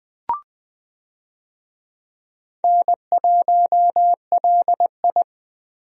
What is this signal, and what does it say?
Morse code, heard as a keyed tone, about 20 wpm, 700 Hz. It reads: N1LI
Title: USA Call Signs - 4 Characters - 20wpm